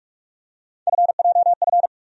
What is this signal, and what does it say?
Morse code, heard as a keyed tone, about 47 wpm, 700 Hz. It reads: FJF
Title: USA Call Sign Postfixes - 3 Letters 45wpm